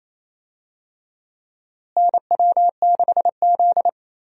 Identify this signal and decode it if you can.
Morse code, heard as a keyed tone, about 27 wpm, 700 Hz. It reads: NW6Z
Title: USA Call Signs - 4 Characters - Repeated 3x - 28wpm